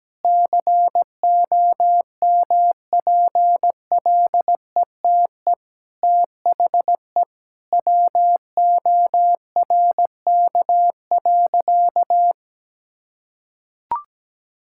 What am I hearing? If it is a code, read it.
Morse code, heard as a keyed tone, about 17 wpm, 700 Hz. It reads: COMPLETE THE WORK.
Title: Sentences from Top 1000 Words - Review Part 6 - 17wpm